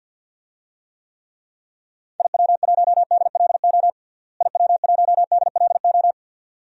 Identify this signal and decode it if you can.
Morse code, heard as a keyed tone, about 49 wpm, 700 Hz. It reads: IW1DLO IW1DLO
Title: DXCC-100 Call Signs - 6 Characters - Repeated 2x - 50wpm